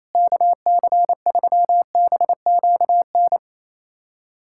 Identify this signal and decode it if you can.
Morse code, heard as a keyed tone, about 28 wpm, 700 Hz. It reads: KC3BQN
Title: USA Callsign DE Callsign - 6 Characters - 28wpm